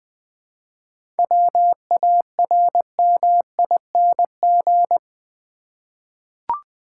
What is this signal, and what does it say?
Morse code, heard as a keyed tone, about 20 wpm, 700 Hz. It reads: WARMING
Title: 7 Letter Words - First 3 Letters Given - 20wpm